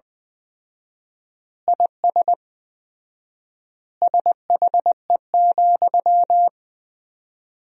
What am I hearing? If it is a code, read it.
Morse code, heard as a keyed tone, about 20 wpm, 700 Hz. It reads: IS SHE,
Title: 4x Word-Spacing - Sentences from Top 200 Words - 20wpm